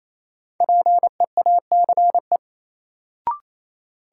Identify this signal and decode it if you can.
Morse code, heard as a keyed tone, about 28 wpm, 700 Hz. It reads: PEACE
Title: Sentences from Top 1500 Words - New Word Drill - 28wpm